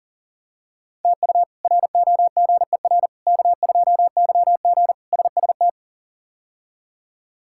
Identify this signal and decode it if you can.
Morse code, heard as a keyed tone, about 37 wpm, 700 Hz. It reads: TU ROGER K2YG SST
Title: K1USN SST QSO Practice - 40wpm